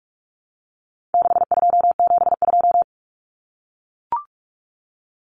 Morse code, heard as a keyed tone, about 44 wpm, 700 Hz. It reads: 6272
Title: Common Cut Numbers - 4-Digit 45wpm